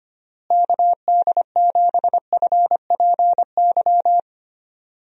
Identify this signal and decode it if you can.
Morse code, heard as a keyed tone, about 25 wpm, 700 Hz. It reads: KD7FPY
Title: USA Call Signs - 6 Characters - Repeated 3x - 25wpm